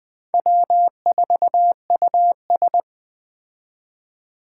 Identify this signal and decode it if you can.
Morse code, heard as a keyed tone, about 21 wpm, 700 Hz. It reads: W4US
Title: Speed Racing - USA 1x2 Call Signs 17wpm